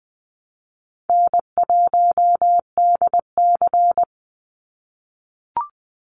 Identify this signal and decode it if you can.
Morse code, heard as a keyed tone, about 20 wpm, 700 Hz. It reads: N1DC
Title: Easy Call Signs - 20wpm